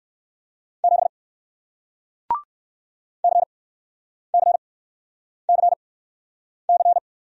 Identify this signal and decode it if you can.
Morse code, heard as a keyed tone, about 60 wpm, spaced as for 9 wpm, 700 Hz. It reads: 7 CCCC